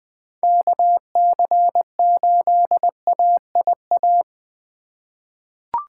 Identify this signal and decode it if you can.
Morse code, heard as a keyed tone, about 20 wpm, 700 Hz. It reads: KC8AIA
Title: USA Call Signs - 6 Characters - 2x Character Spacing - 20wpm